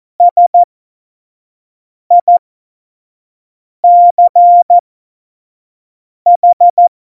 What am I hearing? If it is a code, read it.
Morse code, heard as a keyed tone, about 14 wpm, spaced as for 5 wpm, 700 Hz. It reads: SICH
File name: Morsekurs / Witze / Witz 15. Mai 2018, 25 F70